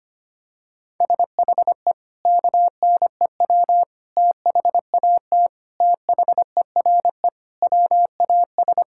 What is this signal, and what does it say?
Morse code, heard as a keyed tone, about 25 wpm, 700 Hz. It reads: SHE KNEW THAT THERE WAS
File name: practice / 25wpm / Sentences with 4-Word Phrases - 25wpm